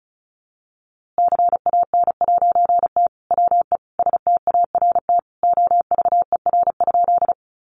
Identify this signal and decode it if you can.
Morse code, heard as a keyed tone, about 34 wpm, 700 Hz. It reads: CAN'T WE START OVER?
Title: A Sound Of Thunder 35wpm